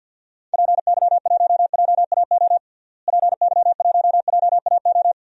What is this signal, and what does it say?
Morse code, heard as a keyed tone, about 50 wpm, 700 Hz. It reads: PY1JAO PY1JAO
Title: DXCC-100 Call Signs - 6 Characters - Repeated 2x - 50wpm